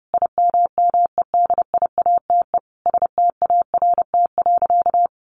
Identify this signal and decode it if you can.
Morse code, heard as a keyed tone, about 30 wpm, 700 Hz. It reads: IMMEDIATE START.
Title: Sentences from Top 1200 Words 30wpm